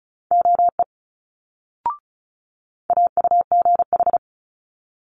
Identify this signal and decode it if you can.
Morse code, heard as a keyed tone, about 36 wpm, 700 Hz. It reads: OE AUGH
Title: Multi-letter Phonemes Quick Fire 35wpm